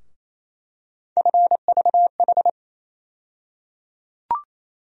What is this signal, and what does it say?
Morse code, heard as a keyed tone, about 28 wpm, 700 Hz. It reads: FVH
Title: USA Call Sign Postfixes - 3 Letters 28wpm